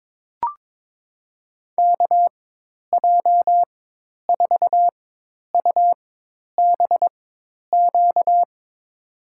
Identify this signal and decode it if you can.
Morse code, heard as a keyed tone, about 21 wpm, spaced as for 10 wpm, 700 Hz. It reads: KJ4UBQ
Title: USA Call Signs - 6 Characters - 4x Character Spacing - 22wpm